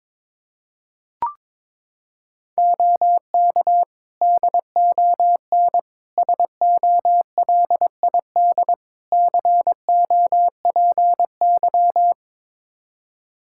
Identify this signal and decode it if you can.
Morse code, heard as a keyed tone, about 22 wpm, 700 Hz. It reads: OK DON SOLID COPY
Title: Practice QSO - Exchange 3 - More Info - 22wpm